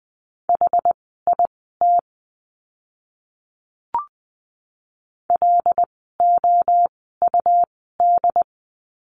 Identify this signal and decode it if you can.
Morse code, heard as a keyed tone, about 20 wpm, spaced as for 14 wpm, 700 Hz. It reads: HIT LOUD
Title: 101 Lesson 17f - Letter u - Words 20wpm